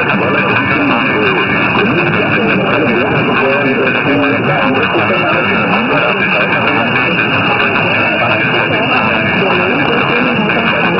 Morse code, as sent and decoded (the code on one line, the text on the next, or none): none